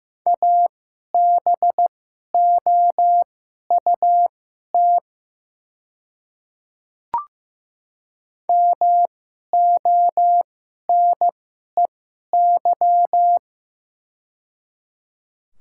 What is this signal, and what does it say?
Morse code, heard as a keyed tone, about 15 wpm, spaced as for 11 wpm, 700 Hz. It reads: ABOUT MONEY
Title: 2x Character Spacing - Top 500 Words - 15wpm